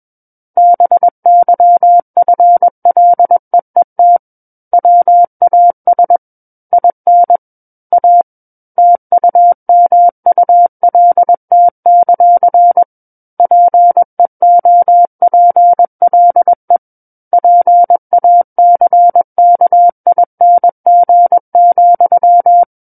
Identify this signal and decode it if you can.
Morse code, heard as a keyed tone, about 21 wpm, 700 Hz. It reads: BYFLEET WAS IN A TUMULT; PEOPLE PACKING,
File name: Book One / 12-Chapter 12 - 21 WPM